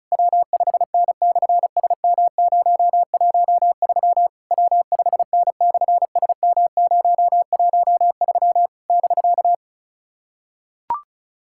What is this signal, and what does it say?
Morse code, heard as a keyed tone, about 33 wpm, 700 Hz. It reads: W5N/SM013 W5N/SM013 <BK>
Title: SOTA QSO Practice 35wpm